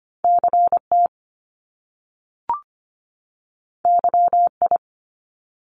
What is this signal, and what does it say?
Morse code, heard as a keyed tone, about 25 wpm, 700 Hz. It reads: CT YI
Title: Top Two Letter Combinations in Words 25wpm